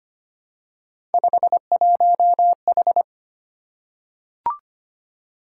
Morse code, heard as a keyed tone, about 25 wpm, 700 Hz. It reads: 51H
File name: Three Letter-Number Combinations - 25wpm